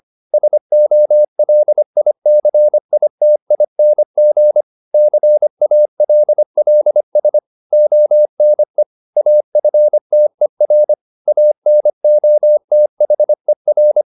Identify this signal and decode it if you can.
Morse code, heard as a keyed tone, about 25 wpm, 600 Hz. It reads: SOLICITING CALLS ONE AFTER ANOTHER